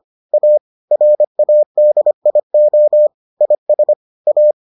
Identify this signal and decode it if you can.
Morse code, heard as a keyed tone, about 25 wpm, 600 Hz. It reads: A RADIO IS A